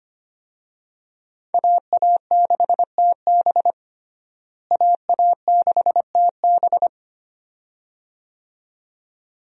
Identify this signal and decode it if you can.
Morse code, heard as a keyed tone, about 25 wpm, 700 Hz. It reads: AA6TB AA6TB
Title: USA Call Signs - 5 Characters - Repeated 2x - 25wpm